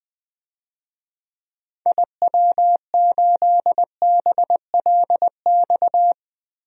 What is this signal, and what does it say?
Morse code, heard as a keyed tone, about 20 wpm, 700 Hz. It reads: IW8BLX
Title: DXCC-100 Call Signs - 6 Characters - Repeated 3x - 20wpm